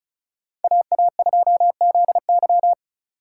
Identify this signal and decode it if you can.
Morse code, heard as a keyed tone, about 36 wpm, 700 Hz. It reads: AA2ZY
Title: USA Call Signs - 5 Characters - 2x Character Spacing - 35wpm